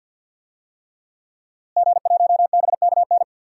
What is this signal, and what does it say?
Morse code, heard as a keyed tone, about 49 wpm, 700 Hz. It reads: G1BKN
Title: DXCC-100 Call Signs - 5 Characters - Repeated 3x - 50wpm